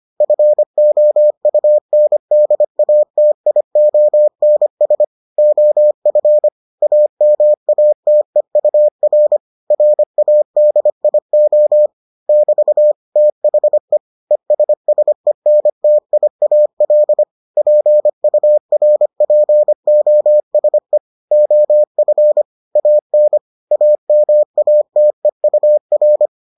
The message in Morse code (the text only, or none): FOUNDATIONS OF AMATEUR RADIO <BT> THE ESSENTIAL PURPOSE OF AN AMATEUR